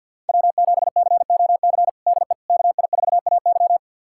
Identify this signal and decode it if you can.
Morse code, heard as a keyed tone, about 46 wpm, 700 Hz. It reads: W7COC DE KI4AY
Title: USA Callsign DE Callsign - 5 Characters - 50wpm